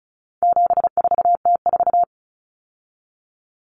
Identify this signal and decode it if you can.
Morse code, heard as a keyed tone, about 34 wpm, 700 Hz. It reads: 74T4
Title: Common Cut Numbers - 4-Digit 35wpm